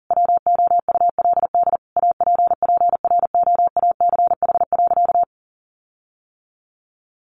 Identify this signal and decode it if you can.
Morse code, heard as a keyed tone, about 40 wpm, 700 Hz. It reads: WOULD APPROACH.
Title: Sentences from Top 1400 Words 40wpm